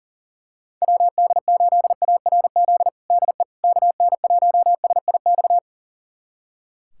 Morse code, heard as a keyed tone, about 40 wpm, 700 Hz. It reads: WD8ARZ DE KN1SIX
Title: USA Callsign DE Callsign - 6 Characters - 40wpm